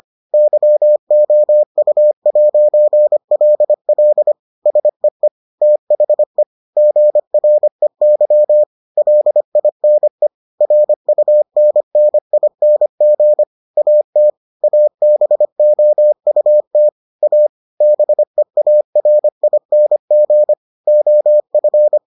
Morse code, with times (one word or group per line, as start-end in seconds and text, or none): 0.3-4.3 YOU'LL
4.7-5.3 SEE
5.6-6.4 THE
6.8-8.6 GREY
9.0-10.3 LINE
10.6-13.4 RUNNING
13.8-14.3 AT
14.6-16.9 ABOUT
17.2-17.5 A
17.8-20.5 BEARING
20.9-22.0 OF